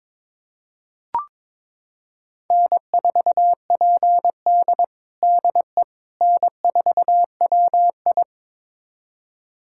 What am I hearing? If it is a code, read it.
Morse code, heard as a keyed tone, about 23 wpm, 700 Hz. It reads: N4PD DE N4WI
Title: USA Callsign DE Callsign - 4 Characters - 22wpm